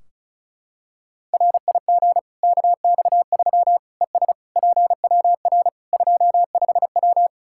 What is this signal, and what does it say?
Morse code, heard as a keyed tone, about 36 wpm, 700 Hz. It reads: RIG KX3 ES PWR 25W
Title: Practice QSO - Exchange 2 - Info - 35wpm